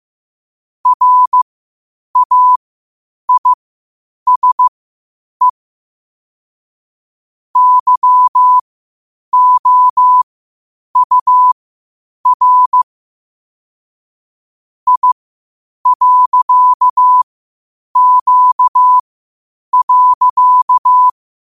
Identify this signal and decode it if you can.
Morse code, heard as a keyed tone, about 15 wpm, spaced as for 8 wpm, 1000 Hz. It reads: RAISE YOUR I.Q.
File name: qotdfiles / 8wpm